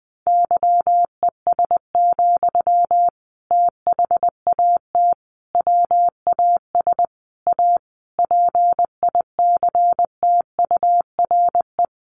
Morse code, heard as a keyed tone, about 20 wpm, 700 Hz. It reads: YES, THAT WAS A PICTURE
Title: Sentences from Top 500 Words 20wpm